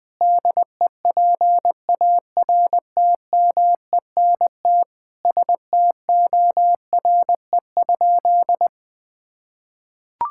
Morse code, heard as a keyed tone, about 21 wpm, 700 Hz. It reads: DEPARTMENT STORE?
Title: Sentences from Top 1000 Words - Review Part 5 - 20wpm